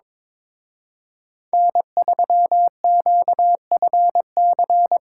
Morse code, heard as a keyed tone, about 24 wpm, 700 Hz. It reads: N3QFC
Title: Speed Racing - USA 1x3 Call Signs 22wpm